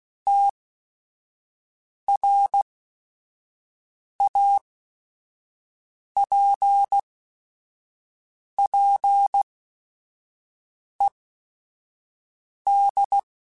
Morse code, heard as a keyed tone, about 16 wpm, spaced as for 5 wpm, 800 Hz. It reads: TRAPPED
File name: QSOs / QSOs-mp3s / kc7sac@5